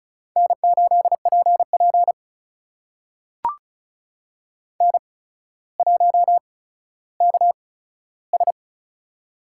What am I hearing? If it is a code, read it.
Morse code, heard as a keyed tone, about 36 wpm, spaced as for 15 wpm, 700 Hz. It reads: N8PP N1KS
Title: USA Call Signs - 4 Characters - 8x Character Spacing - 35wpm